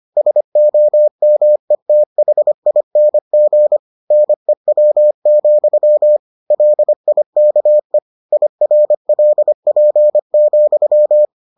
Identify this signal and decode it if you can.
Morse code, heard as a keyed tone, about 25 wpm, 600 Hz. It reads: SOMETHING NEW, LIKE IRLP,